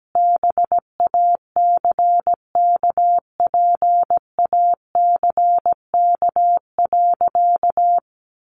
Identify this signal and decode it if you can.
Morse code, heard as a keyed tone, about 17 wpm, 700 Hz. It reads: BACKPACK.
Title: Additional Easy Phrases 17wpm